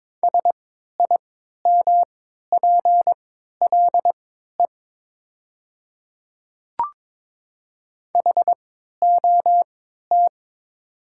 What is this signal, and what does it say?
Morse code, heard as a keyed tone, about 22 wpm, spaced as for 13 wpm, 700 Hz. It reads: SIMPLE HOT
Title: 3x Character Spacing - Top 500 Words - 22wpm